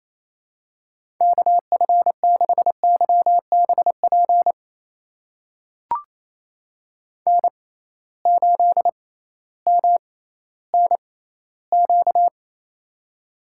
Morse code, heard as a keyed tone, about 27 wpm, spaced as for 15 wpm, 700 Hz. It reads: KF6YBP N8MNQ
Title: USA Call Signs - 6x Character Spacing - 28wpm